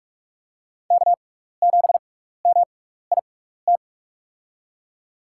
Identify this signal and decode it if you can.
Morse code, heard as a keyed tone, about 44 wpm, spaced as for 15 wpm, 700 Hz. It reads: K7MIT
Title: USA Call Signs - 5 Characters - 6x Character Spacing - 45wpm